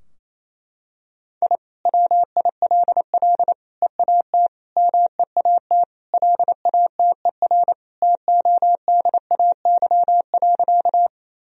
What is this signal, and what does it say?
Morse code, heard as a keyed tone, about 27 wpm, 700 Hz. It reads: I WILL EAT MEAT LATER TODAY.
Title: Sentences from Top 1000 Words - Review Part 3 - 28wpm